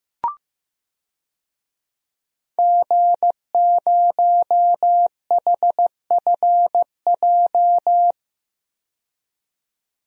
Morse code, heard as a keyed tone, about 15 wpm, 700 Hz. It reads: G0HFJ
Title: DXCC-100 Call Signs - 5 Characters - 15wpm